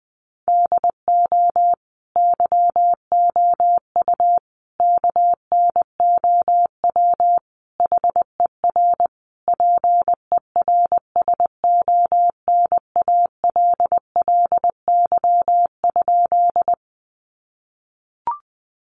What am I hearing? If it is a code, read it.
Morse code, heard as a keyed tone, about 20 wpm, 700 Hz. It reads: DO YOU KNOW HER PERSONALLY?
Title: Sentences from Top 1400 Words 20wpm